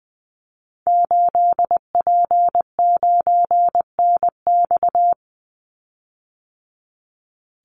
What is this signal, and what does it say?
Morse code, heard as a keyed tone, about 20 wpm, 700 Hz. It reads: 8P9NX
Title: Call Signs - 20wpm